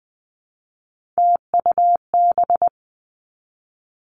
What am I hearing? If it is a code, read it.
Morse code, heard as a keyed tone, about 20 wpm, 700 Hz. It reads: TUB
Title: Three Challenging Letter-Number Combinations 20wpm